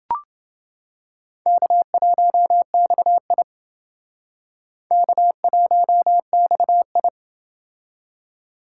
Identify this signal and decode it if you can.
Morse code, heard as a keyed tone, about 28 wpm, 700 Hz. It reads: K1XI K1XI